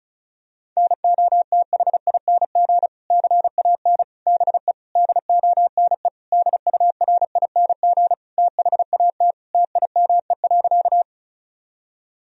Morse code, heard as a keyed tone, about 34 wpm, 700 Hz. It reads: NOTHING CAN BE DONE DURING THAT TIME.
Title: Sentences from Top 1000 Words - Review Part 2 - 35wpm